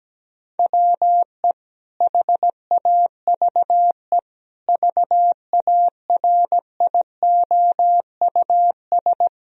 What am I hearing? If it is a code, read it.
Morse code, heard as a keyed tone, about 17 wpm, 700 Hz. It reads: WE HAVE VARIOUS
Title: Sentences from Top 1000 Words - Review Part 4 - 17wpm